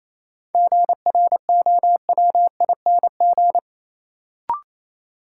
Morse code, heard as a keyed tone, about 28 wpm, 700 Hz. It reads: GROWING
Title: Sets of 5 Words Encore - New Word Drill - 28wpm